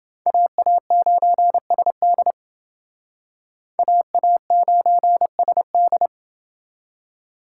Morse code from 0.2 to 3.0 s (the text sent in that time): AA9SD